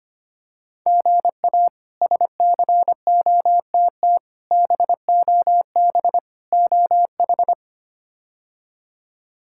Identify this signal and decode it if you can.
Morse code, heard as a keyed tone, about 25 wpm, 700 Hz. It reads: GA SCOTT BOB OH
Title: K1USN SST QSO Practice - 25wpm